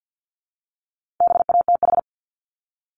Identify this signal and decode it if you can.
Morse code, heard as a keyed tone, about 50 wpm, 700 Hz. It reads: BATH
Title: Multi-letter Phonemes Repeated 3x 50wpm